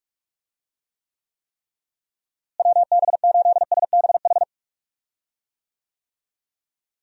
Morse code, heard as a keyed tone, about 46 wpm, 700 Hz. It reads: WB8SBH
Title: USA Call Signs - 6 Characters - Repeated 3x - 45wpm